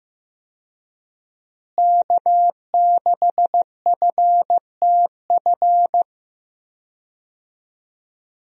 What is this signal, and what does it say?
Morse code, heard as a keyed tone, about 15 wpm, 700 Hz. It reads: K6FTF
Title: USA Call Signs - 5 Characters - Repeated 3x - 15wpm